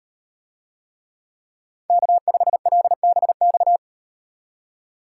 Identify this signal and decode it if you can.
Morse code, heard as a keyed tone, about 38 wpm, 700 Hz. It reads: K5LBX